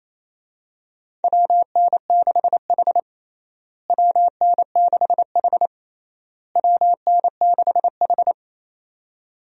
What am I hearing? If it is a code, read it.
Morse code, heard as a keyed tone, about 28 wpm, 700 Hz. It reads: WN6H WN6H WN6H
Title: USA Call Signs - 4 Characters - Repeated 3x - 28wpm